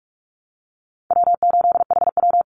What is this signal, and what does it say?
Morse code, heard as a keyed tone, about 44 wpm, 700 Hz. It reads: W8HW
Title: Easy Call Signs Repeated 2x - 45wpm